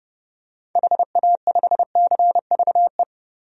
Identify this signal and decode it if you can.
Morse code, heard as a keyed tone, about 30 wpm, 700 Hz. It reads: HA5CVE
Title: DXCC-100 Call Signs - 6 Characters - Repeated 2x - 30wpm